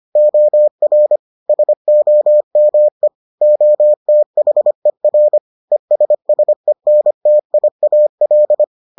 Morse code, heard as a keyed tone, about 25 wpm, 600 Hz. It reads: OR SOME OTHER ESSENTIAL